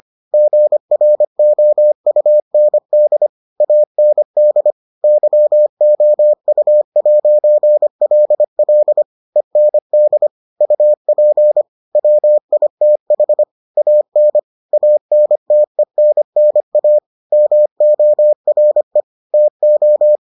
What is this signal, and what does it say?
Morse code, heard as a keyed tone, about 25 wpm, 600 Hz. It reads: GROUND AND YOU'LL END UP WITH AN ANTENNA MORE TO